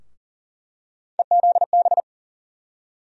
Morse code, heard as a keyed tone, about 40 wpm, 700 Hz. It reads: EZB
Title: USA Call Sign Postfixes - 3 Letters 40wpm